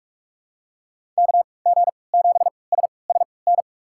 Morse code, heard as a keyed tone, about 44 wpm, spaced as for 25 wpm, 700 Hz. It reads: KG7SSN
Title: USA Call Signs - 6 Characters - 3x Character Spacing - 45wpm